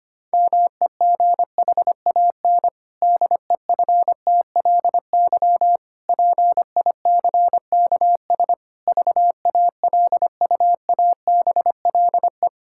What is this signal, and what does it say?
Morse code, heard as a keyed tone, about 25 wpm, 700 Hz. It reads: MEGHAN DEFTLY PICKS VALUABLE
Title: Pangram Challenge v2 - 25wpm